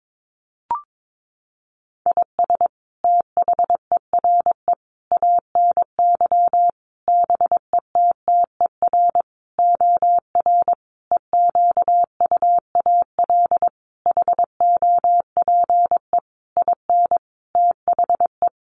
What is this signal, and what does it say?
Morse code, heard as a keyed tone, about 22 wpm, 700 Hz. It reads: IS THERE ANY BETTER OR EQUAL HOPE IN THE
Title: Sentences from Top 1400 Words 22wpm